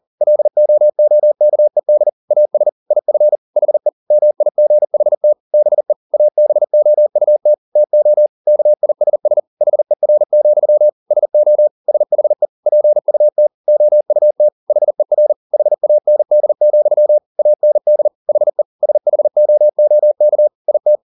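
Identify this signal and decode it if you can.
Morse code, heard as a keyed tone, about 40 wpm, 600 Hz. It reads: LOOKED AS IF HE MIGHT BE ABOUT TO KISS HER, SO SHE PUT OUT HER HAND, AND HE SHOOK IT
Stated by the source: Morse code at 40 wpm, practice text, CW med Gnister.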